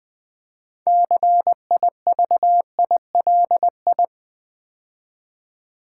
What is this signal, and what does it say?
Morse code, heard as a keyed tone, about 20 wpm, 700 Hz. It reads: CIVILI
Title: Build A Word - 2 Letters At A Time - 8-Letter Words - 20wpm